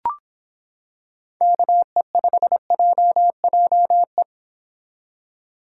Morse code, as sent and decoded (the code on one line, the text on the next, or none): -.- . ..... .--- .--- .
KE5JJE